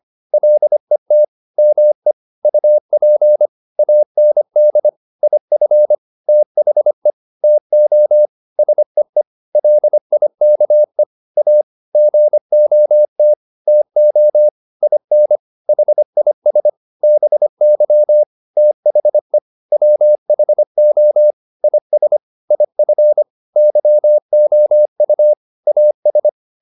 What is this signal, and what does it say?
Morse code, heard as a keyed tone, about 25 wpm, 600 Hz. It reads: LET ME UP AND IF THE TO SEE LIKE A GOT TO IN HIS BY THE WHO IS IF YOU AS